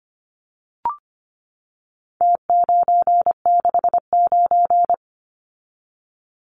Morse code, heard as a keyed tone, about 25 wpm, 700 Hz. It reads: T969
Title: Numbers - Contest Style - 4-Digit 25wpm